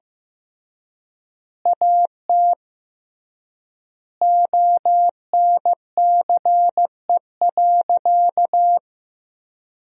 Morse code, heard as a keyed tone, about 15 wpm, 700 Hz. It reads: AT ONCE.
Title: 3x Word-Spacing - Sentences from Top 500 Words - 15wpm